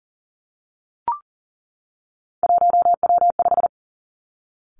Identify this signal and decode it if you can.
Morse code, heard as a keyed tone, about 39 wpm, 700 Hz. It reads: 1W5
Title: Three Challenging Letter-Number Combinations 40wpm